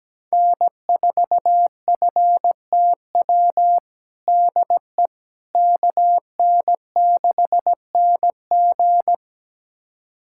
Morse code, heard as a keyed tone, about 17 wpm, 700 Hz. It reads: N4FTW DE KN6NG
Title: USA Callsign DE Callsign - 5 Characters - 17wpm